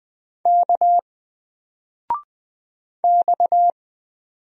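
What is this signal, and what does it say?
Morse code, heard as a keyed tone, about 20 wpm, 700 Hz. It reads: K X
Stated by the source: Single letter and number Morse drill at 20 wpm, from N4LCD.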